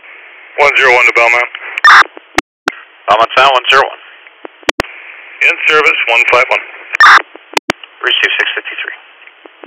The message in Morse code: none

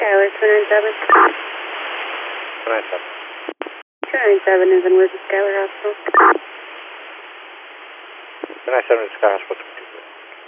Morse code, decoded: none

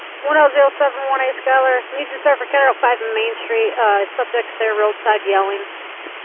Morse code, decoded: none

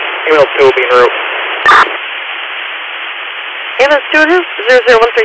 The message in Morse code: none

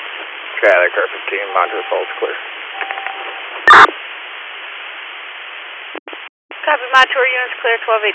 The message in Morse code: none

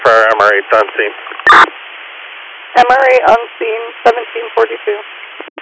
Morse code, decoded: none

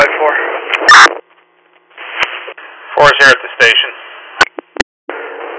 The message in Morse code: none

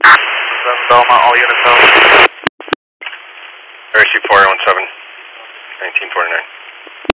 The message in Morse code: none

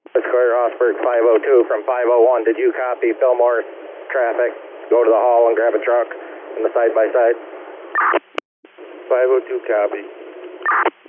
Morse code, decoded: none

